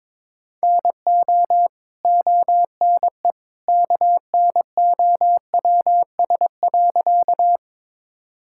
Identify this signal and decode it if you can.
Morse code, heard as a keyed tone, about 22 wpm, 700 Hz. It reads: NO ONE KNOWS.
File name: practice / 22wpm / Sentences with 2-Word Phrases - 22wpm